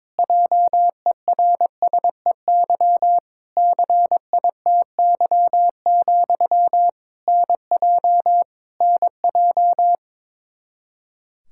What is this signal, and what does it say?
Morse code, heard as a keyed tone, about 22 wpm, 700 Hz. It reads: JERSEY CITY, NJ NJ
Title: Practice QSO - Exchange 1 - Beginning - 22wpm